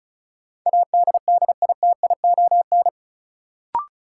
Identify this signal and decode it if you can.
Morse code, heard as a keyed tone, about 34 wpm, 700 Hz. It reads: ADDITION
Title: Sentences from Top 1500 Words - New Word Drill Alt - 35wpm